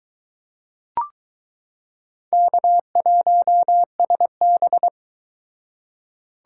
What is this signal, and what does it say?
Morse code, heard as a keyed tone, about 22 wpm, 700 Hz. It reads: K1SB